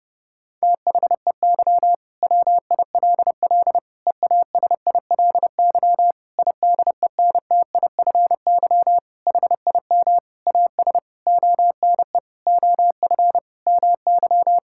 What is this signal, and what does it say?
Morse code, heard as a keyed tone, about 30 wpm, 700 Hz. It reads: THEY WILL EASILY IDENTIFY HIM AS ONE OF MY
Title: Sentences from Top 1000 Words - Review Part 6 - 30wpm